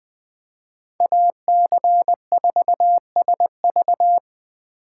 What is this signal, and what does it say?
Morse code, heard as a keyed tone, about 22 wpm, 700 Hz. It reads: AC4SV
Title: Speed Racing - USA 2x2 Call Signs 20wpm